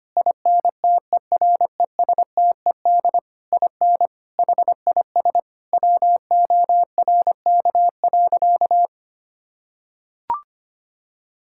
Morse code, heard as a keyed tone, about 25 wpm, 700 Hz. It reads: INTERESTED IN HIS WORK.
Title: Sentences from Top 1000 Words - Review Part 6 - 25wpm